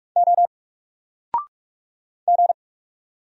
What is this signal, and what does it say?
Morse code, heard as a keyed tone, about 44 wpm, 700 Hz. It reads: O G
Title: Single Character (Letter:Number:Punctuation:Prosign) 45wpm